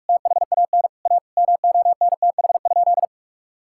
Morse code, heard as a keyed tone, about 46 wpm, 700 Hz. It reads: THAN A MONTH?
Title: Sentences from Top 1000 Words - Review Part 6 - 45wpm